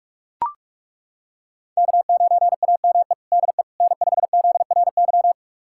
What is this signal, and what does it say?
Morse code, heard as a keyed tone, about 42 wpm, 700 Hz. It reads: K9AME DE N5ZRY
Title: USA Callsign DE Callsign - 5 Characters - 45wpm